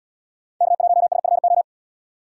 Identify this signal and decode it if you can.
Morse code, heard as a keyed tone, about 77 wpm, 700 Hz. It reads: K1SFO